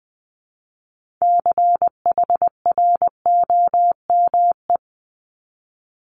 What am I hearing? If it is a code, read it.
Morse code, heard as a keyed tone, about 20 wpm, 700 Hz. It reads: CHROME
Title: Multi-letter Phonemes 20wpm